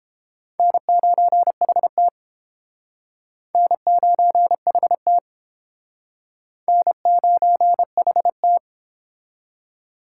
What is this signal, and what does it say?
Morse code, heard as a keyed tone, about 31 wpm, 700 Hz. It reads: N9HT N9HT N9HT